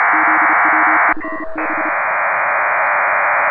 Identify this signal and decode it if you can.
Morse code, heard as a keyed tone, about 32 wpm, 310 Hz, with noise. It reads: ZW5B
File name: Pile-up with PACTOR3